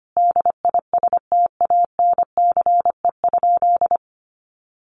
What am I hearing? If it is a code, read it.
Morse code, heard as a keyed tone, about 25 wpm, 700 Hz. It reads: DISTANCE?
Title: Sentences from Top 1200 Words 25wpm